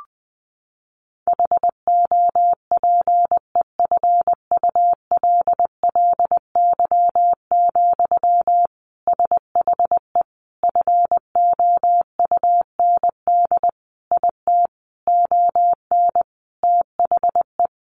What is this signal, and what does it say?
Morse code, heard as a keyed tone, about 20 wpm, 700 Hz. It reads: HOPEFULLY, SHE FOUND IT ON THE
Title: Sentences from Top 1500 Words 20wpm